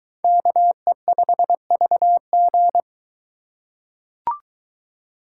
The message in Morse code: KE5VG

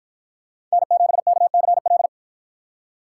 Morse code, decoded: N7KCL

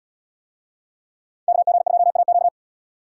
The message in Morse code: KG2AQ

